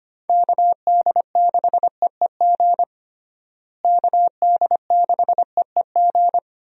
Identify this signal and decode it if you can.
Morse code, heard as a keyed tone, about 25 wpm, 700 Hz. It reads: KD6EEG KD6EEG
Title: USA Call Signs - 6 Characters - Repeated 2x - 25wpm